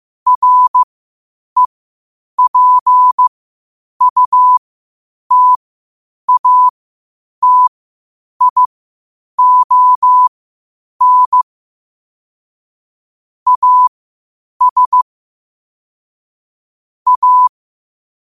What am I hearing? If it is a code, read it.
Morse code, heard as a keyed tone, about 15 wpm, spaced as for 8 wpm, 1000 Hz. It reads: REPUTATION AS A
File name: qotdfiles / 8wpm